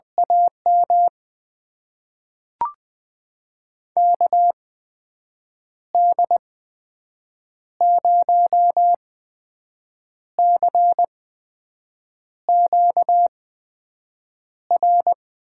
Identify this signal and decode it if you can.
Morse code, heard as a keyed tone, about 20 wpm, spaced as for 6 wpm, 700 Hz. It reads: AM KD0CQR
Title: USA Call Signs - 6 Characters - 8x Character Spacing - 20wpm